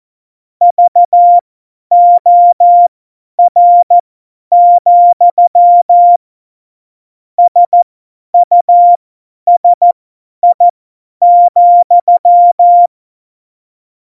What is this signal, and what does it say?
Morse code, heard as a keyed tone, about 14 wpm, spaced as for 10 wpm, 700 Hz. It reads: VOR, SUSI,
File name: Morsekurs / Witze / Witz 05 Mai 2020, 50 F70